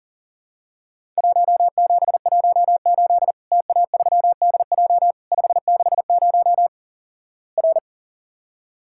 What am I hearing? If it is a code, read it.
Morse code, heard as a keyed tone, about 37 wpm, 700 Hz. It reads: 1718 TA3DJ 560 R
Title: Worked All Europe DX Contest - QTC Exchange - 40wpm